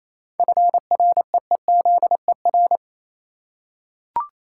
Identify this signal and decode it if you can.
Morse code, heard as a keyed tone, about 27 wpm, 690 Hz. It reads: FREEZER
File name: Easy Words - Speed-Racing with Hints - 22wpm